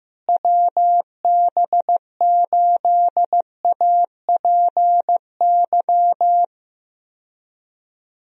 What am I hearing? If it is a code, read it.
Morse code, heard as a keyed tone, about 15 wpm, 700 Hz. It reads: WB8APY
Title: USA Call Signs - 6 Characters - Repeated 2x - 15wpm